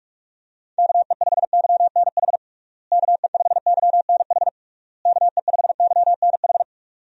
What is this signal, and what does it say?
Morse code, heard as a keyed tone, about 41 wpm, 700 Hz. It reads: KE5YNH KE5YNH KE5YNH
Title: USA Call Signs - 6 Characters - Repeated 3x - 45wpm